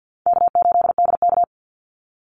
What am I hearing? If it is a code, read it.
Morse code, heard as a keyed tone, about 50 wpm, 700 Hz. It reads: K8DK
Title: Easy Call Signs Repeated 3x - 50wpm